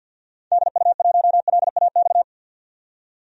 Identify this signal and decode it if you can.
Morse code, heard as a keyed tone, about 50 wpm, 700 Hz. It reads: DU1LAX
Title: DXCC-100 Call Signs - 6 Characters - 50wpm